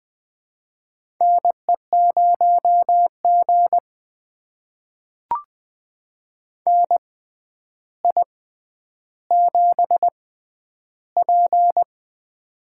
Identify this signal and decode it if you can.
Morse code, heard as a keyed tone, about 20 wpm, spaced as for 10 wpm, 700 Hz. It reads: NE0G NI7P
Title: USA Call Signs - 4 Characters - 6x Character Spacing - 20wpm